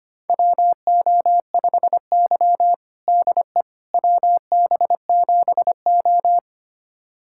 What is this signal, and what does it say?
Morse code, heard as a keyed tone, about 25 wpm, 700 Hz. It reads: WO5Y DE WB7O
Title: USA Callsign DE Callsign - 4 Characters - 25wpm